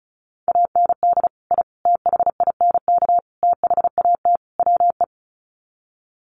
Morse code, heard as a keyed tone, about 34 wpm, 700 Hz. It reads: AND I THINK THAT WE
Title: Sets of 5 Words Encore2 35wpm